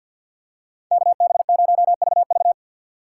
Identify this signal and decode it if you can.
Morse code, heard as a keyed tone, about 47 wpm, 700 Hz. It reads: KB0VV
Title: USA Call Signs - 5 Characters - 6x Character Spacing - 50wpm